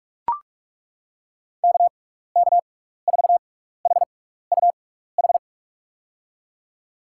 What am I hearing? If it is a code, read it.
Morse code, heard as a keyed tone, about 42 wpm, spaced as for 15 wpm, 700 Hz. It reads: KK4HUH
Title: USA Call Signs - 6x Character Spacing - 45wpm